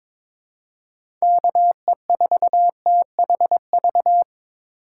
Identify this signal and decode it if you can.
Morse code, heard as a keyed tone, about 22 wpm, 700 Hz. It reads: KE4THV
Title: USA Call Signs - 6 Characters - 2x Character Spacing - 22wpm